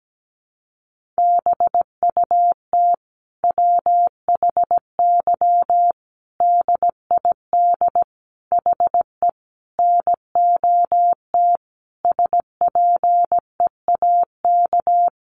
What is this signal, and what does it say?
Morse code, heard as a keyed tone, about 17 wpm, 700 Hz. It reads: BUT WHY DID HE NOT SPEAK
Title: Sentences from Top 1300 Words 17wpm